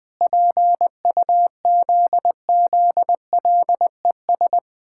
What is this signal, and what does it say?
Morse code, heard as a keyed tone, about 20 wpm, 700 Hz. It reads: PUZZLES
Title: Pangram Challenge v2 - 20wpm